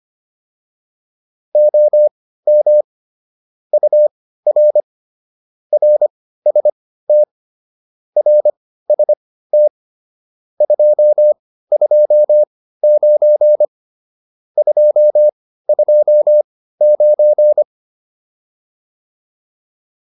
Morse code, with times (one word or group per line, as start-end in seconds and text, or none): 1.6-2.8 OM
3.7-4.8 UR
5.7-7.2 RST
8.2-9.7 RST
10.6-13.7 229
14.6-17.6 229